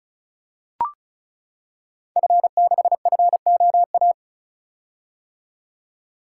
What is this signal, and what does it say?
Morse code, heard as a keyed tone, about 36 wpm, 690 Hz. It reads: F6FOA
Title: Code Groups - 5 Letter:Numbers - 35wpm